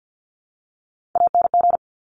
Morse code, heard as a keyed tone, about 50 wpm, 700 Hz. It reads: ANG
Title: Top Three Letter Combinations in Words 50wpm